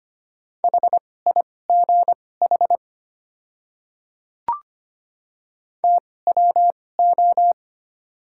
Morse code, heard as a keyed tone, about 25 wpm, spaced as for 18 wpm, 690 Hz. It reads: HIGH TWO
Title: 2x Character Spacing - Top 200 Words - 25wpm